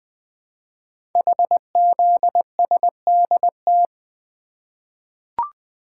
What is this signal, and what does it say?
Morse code, heard as a keyed tone, about 20 wpm, 700 Hz. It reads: HZSDT
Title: Code Groups - 5 Letters - 20wpm